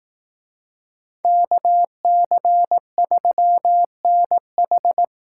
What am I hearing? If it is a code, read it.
Morse code, heard as a keyed tone, about 18 wpm, 700 Hz. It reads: KC3NH
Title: Speed Racing - USA 2x2 Call Signs 15wpm